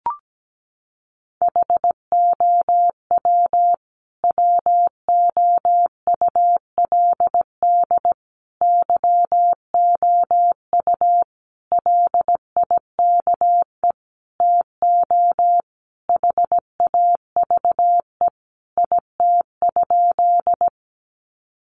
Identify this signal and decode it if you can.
Morse code, heard as a keyed tone, about 17 wpm, 700 Hz. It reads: HOW WOULD YOU LIKE TO HAVE IT?
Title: Sentences from Top 100 Words 17wpm